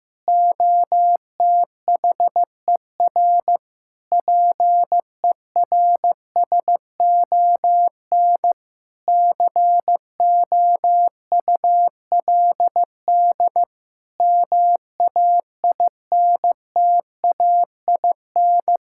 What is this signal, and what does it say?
Morse code, heard as a keyed tone, about 15 wpm, 700 Hz. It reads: OTHER PERSON COULD MAINTAIN
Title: Sentences from Top 1000 Words - Review Part 5 - 15wpm